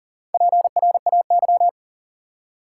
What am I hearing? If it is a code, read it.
Morse code, heard as a keyed tone, about 40 wpm, 700 Hz. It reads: PRAY
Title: Four Letter Words - 40wpm